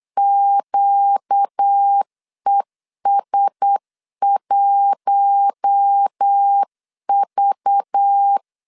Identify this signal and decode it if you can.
Morse code, heard as a keyed tone, about 8 wpm, 800 Hz. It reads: QES1V
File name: CW / DET / 1 9 1500C S A 8wpm 5637